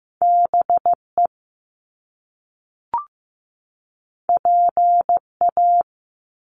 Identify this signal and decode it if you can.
Morse code, heard as a keyed tone, about 15 wpm, 700 Hz. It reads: BE PA
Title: Top Two Letter Combinations in Words 15wpm